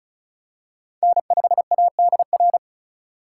Morse code, heard as a keyed tone, about 36 wpm, 700 Hz. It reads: N5ADR
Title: USA Call Signs - 5 Characters - 4x Character Spacing - 35wpm